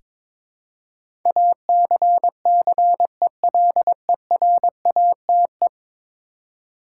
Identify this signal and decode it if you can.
Morse code, heard as a keyed tone, about 22 wpm, 700 Hz. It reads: ACCELERATE
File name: practice / 22wpm / 10 Letter Words - First 3 Letters Given - 22wpm